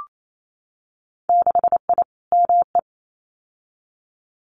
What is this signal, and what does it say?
Morse code, heard as a keyed tone, about 28 wpm, 700 Hz. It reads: 6I ME
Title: Winter Field Day Exchanges Encore 28wpm